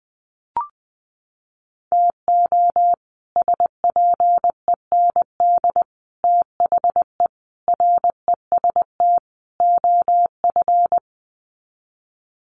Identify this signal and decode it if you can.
Morse code, heard as a keyed tone, about 20 wpm, 700 Hz. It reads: TO SPEND THE REST OF
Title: Sets of 5 Words Encore2 20wpm